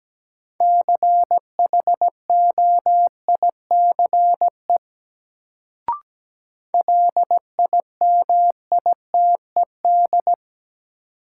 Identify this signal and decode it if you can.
Morse code, heard as a keyed tone, about 17 wpm, 700 Hz. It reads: CHOICE LIMITED
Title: Sentences from Top 900 Words - New Word Drill - 17wpm